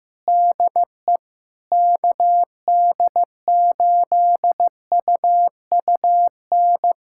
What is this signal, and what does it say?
Morse code, heard as a keyed tone, about 16 wpm, 700 Hz. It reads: DE KD8UUN
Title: USA Callsign DE Callsign - 6 Characters - 15wpm